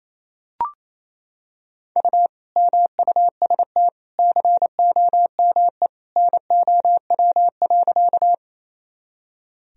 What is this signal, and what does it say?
Morse code, heard as a keyed tone, about 28 wpm, 700 Hz. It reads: U MUST COME NOW.
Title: Sentences from Top 200 Words and Abbreviations 28wpm